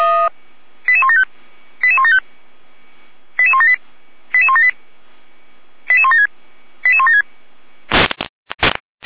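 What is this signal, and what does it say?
Five-tone selective call (ZVEI), heard as quick runs of five tones, tones of about 80 ms. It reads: 88175 88175 88178 88178 88176 88176
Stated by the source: ZVEI five-tone call received on SDR, logged by Feuerwehr Orlamünde.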